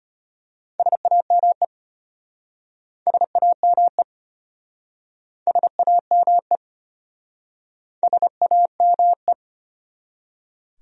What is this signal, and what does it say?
Morse code, heard as a keyed tone, about 33 wpm, 700 Hz. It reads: SAME SAME SAME SAME